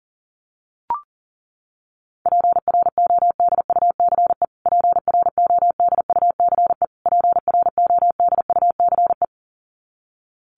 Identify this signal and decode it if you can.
Morse code, heard as a keyed tone, about 40 wpm, 700 Hz. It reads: PRODUCE PRODUCE PRODUCE
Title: Top 500 Words Repeated 40wpm